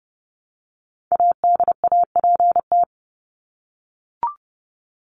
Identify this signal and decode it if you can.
Morse code, heard as a keyed tone, about 30 wpm, 700 Hz. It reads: ADAPT
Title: Top 1501-1750 Words 30wpm